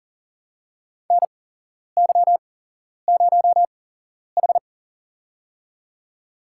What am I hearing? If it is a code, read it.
Morse code, heard as a keyed tone, about 40 wpm, spaced as for 11 wpm, 700 Hz. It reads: NY0H
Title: USA Call Signs - 4 Characters - 8x Character Spacing - 40wpm